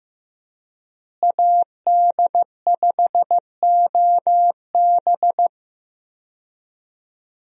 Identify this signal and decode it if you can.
Morse code, heard as a keyed tone, about 16 wpm, 700 Hz. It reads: AD5OB
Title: USA Call Signs - 5 Characters - Repeated 2x - 15wpm